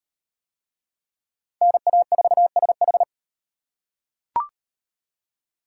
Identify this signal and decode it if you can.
Morse code, heard as a keyed tone, about 38 wpm, 700 Hz. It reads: NA4SH